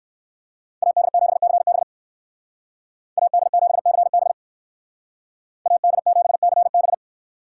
Morse code, heard as a keyed tone, about 58 wpm, 700 Hz. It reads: AD7CB AD7CB AD7CB